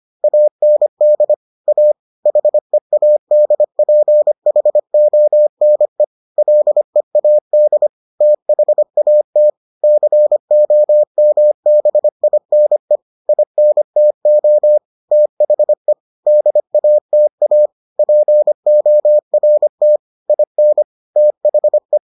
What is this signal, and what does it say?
Morse code, heard as a keyed tone, about 25 wpm, 600 Hz. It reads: AND A HEADPHONE LEAD THAT COMBINE INTO THE DATA PORT IN THE